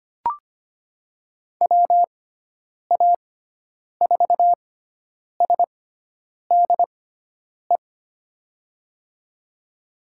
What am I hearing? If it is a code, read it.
Morse code, heard as a keyed tone, about 25 wpm, spaced as for 9 wpm, 700 Hz. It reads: WA4SDE
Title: USA Call Signs - 6 Characters - 6x Character Spacing - 25wpm